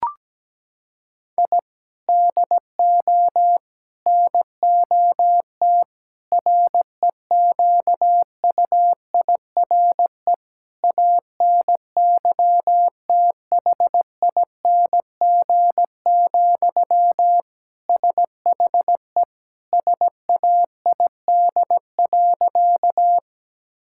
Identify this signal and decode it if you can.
Morse code, heard as a keyed tone, about 17 wpm, 700 Hz. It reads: I DO NOT REQUIRE ANYTHING, SHE SAID.
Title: Sentences from Top 1000 Words - Review Part 6 - 17wpm